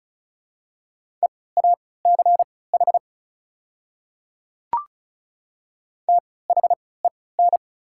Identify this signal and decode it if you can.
Morse code, heard as a keyed tone, about 35 wpm, spaced as for 20 wpm, 700 Hz. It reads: EACH THEN
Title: 3x Character Spacing - Top 100 Words - 35wpm